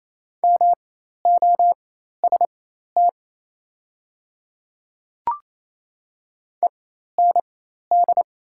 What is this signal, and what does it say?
Morse code, heard as a keyed tone, about 28 wpm, spaced as for 13 wpm, 700 Hz. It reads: MOST END
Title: 4x Character Spacing - Top 300 Words - 28wpm